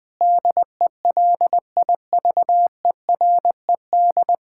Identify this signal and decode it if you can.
Morse code, heard as a keyed tone, about 21 wpm, 700 Hz. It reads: DELIVERED
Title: Pangram Challenge v2 - 20wpm